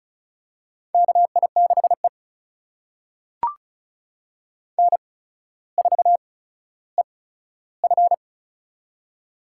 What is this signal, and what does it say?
Morse code, heard as a keyed tone, about 36 wpm, spaced as for 15 wpm, 700 Hz. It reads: KI6E N4EF
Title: USA Call Signs - 4 Characters - 8x Character Spacing - 35wpm